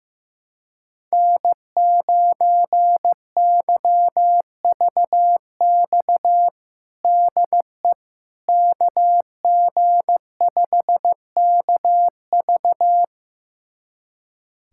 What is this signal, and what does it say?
Morse code, heard as a keyed tone, about 15 wpm, 700 Hz. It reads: N9YVX DE KG5KV
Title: USA Callsign DE Callsign - 5 Characters - 15wpm